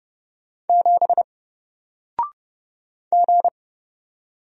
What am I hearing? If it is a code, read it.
Morse code, heard as a keyed tone, about 30 wpm, 700 Hz. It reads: 7 G
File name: Single-Letter-Number-30wpm